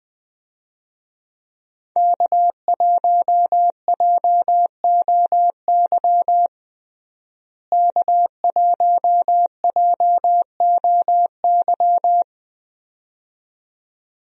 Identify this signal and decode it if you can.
Morse code, heard as a keyed tone, about 20 wpm, 700 Hz. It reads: K1JOY K1JOY
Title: USA Call Signs - 5 Characters - Repeated 2x - 20wpm